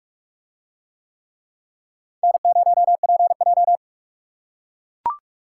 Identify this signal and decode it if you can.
Morse code, heard as a keyed tone, about 42 wpm, 700 Hz. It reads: N0PJ